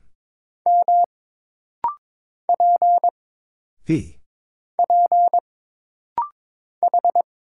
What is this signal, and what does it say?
Morse code, heard as a keyed tone, about 22 wpm, 700 Hz. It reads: M P P H
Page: This rapid-fire timing is geared to push your ICR (Instant Character Recognition) skills.